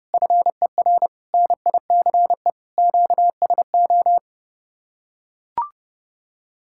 Morse code, heard as a keyed tone, about 30 wpm, 700 Hz. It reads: FER NICE QSO